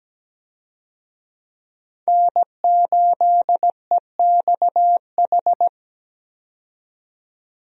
Morse code, heard as a keyed tone, about 17 wpm, 700 Hz. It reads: N8EXH
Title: USA Call Signs - 5 Characters - 17wpm